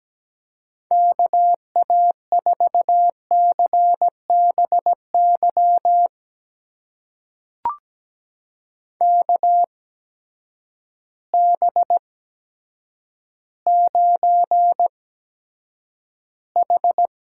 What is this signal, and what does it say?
Morse code, heard as a keyed tone, about 17 wpm, spaced as for 8 wpm, 700 Hz. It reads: KA4CBY KB9H
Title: USA Call Signs - 6 Characters - 8x Character Spacing - 17wpm